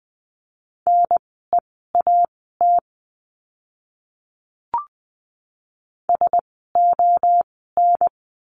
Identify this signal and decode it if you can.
Morse code, heard as a keyed tone, about 20 wpm, spaced as for 14 wpm, 700 Hz. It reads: NEAT SON
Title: 054 Lesson 10f - Letter r - Words 20wpm